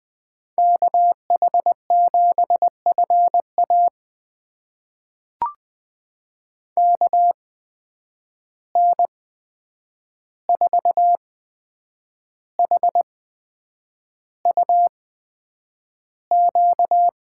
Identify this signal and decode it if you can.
Morse code, heard as a keyed tone, about 20 wpm, spaced as for 8 wpm, 700 Hz. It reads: KH7FA KN4HUQ
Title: USA Call Signs - 8x Character Spacing - 20wpm